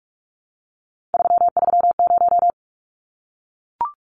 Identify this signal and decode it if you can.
Morse code, heard as a keyed tone, about 44 wpm, 700 Hz. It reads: 330
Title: Numbers - 3-Digit 45wpm